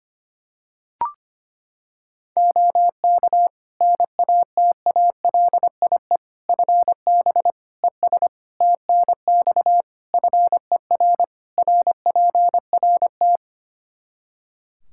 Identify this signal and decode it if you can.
Morse code, heard as a keyed tone, about 25 wpm, 700 Hz. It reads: OK NATALIE FB ES TNX FER RPRT
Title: Practice QSO - Exchange 2 - Info - 25wpm